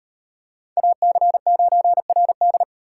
Morse code, heard as a keyed tone, about 38 wpm, 700 Hz. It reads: AC9RD